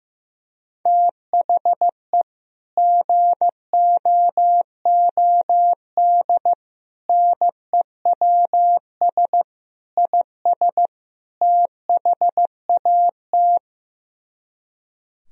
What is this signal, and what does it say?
Morse code, heard as a keyed tone, about 15 wpm, 700 Hz. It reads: THE GOOD NEWS IS THAT
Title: Speed-Racing - Sets of 5 Words 15wpm